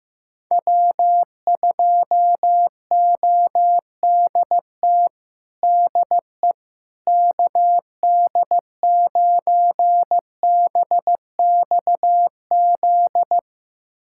Morse code, heard as a keyed tone, about 16 wpm, 700 Hz. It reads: W2ODT DE KD9BXZ
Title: USA Callsign DE Callsign 15wpm